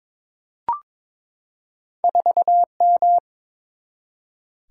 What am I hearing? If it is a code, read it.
Morse code, heard as a keyed tone, about 23 wpm, 700 Hz. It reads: 4M
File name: practice / 22wpm / Two Letter-Number Combinations - 22wpm